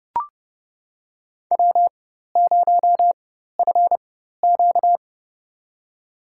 Morse code, heard as a keyed tone, about 30 wpm, spaced as for 14 wpm, 700 Hz. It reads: W0FQ
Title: USA Call Signs - 4 Characters - 4x Character Spacing - 30wpm